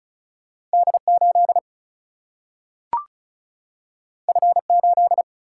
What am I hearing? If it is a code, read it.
Morse code, heard as a keyed tone, about 36 wpm, 700 Hz. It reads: D8 F8
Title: Single-Letter Single-Number 35wpm